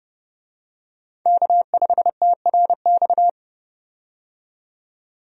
Morse code, heard as a keyed tone, about 30 wpm, 700 Hz. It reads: K5TRX
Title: USA Call Signs - 5 Characters - 30wpm